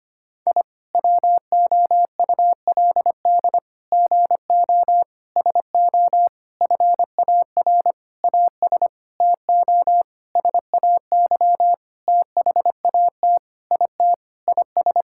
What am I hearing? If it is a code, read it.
Morse code, heard as a keyed tone, about 25 wpm, 700 Hz. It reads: I WOULD GO SO FAR AS TO SAY THAT IT IS
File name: practice / 25wpm / Sentences with 9-Word Phrases - 25wpm